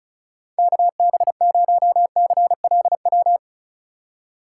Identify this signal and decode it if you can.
Morse code, heard as a keyed tone, about 34 wpm, 700 Hz. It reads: KB0CLW
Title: USA Call Signs - 2x Character Spacing - 35wpm